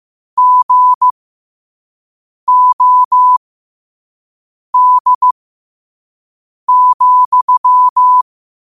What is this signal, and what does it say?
Morse code, heard as a keyed tone, about 15 wpm, spaced as for 5 wpm, 1000 Hz. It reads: GOD,
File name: qotdfiles / 5wpm